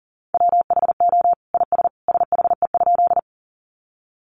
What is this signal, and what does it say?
Morse code, heard as a keyed tone, about 40 wpm, 700 Hz. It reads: WHO IS SHE?
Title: Sentences from Top 100 Words 40wpm